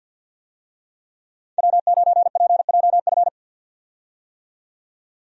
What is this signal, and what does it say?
Morse code, heard as a keyed tone, about 50 wpm, 700 Hz. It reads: W9PJF
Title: USA Call Signs - 5 Characters - 50wpm